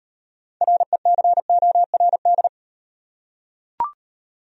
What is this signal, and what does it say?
Morse code, heard as a keyed tone, about 38 wpm, 700 Hz. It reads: RECORD